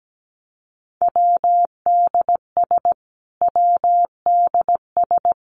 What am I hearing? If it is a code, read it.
Morse code, heard as a keyed tone, about 17 wpm, 700 Hz. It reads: WDS WDS
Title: Abbreviations Repeated 3x - 17wpm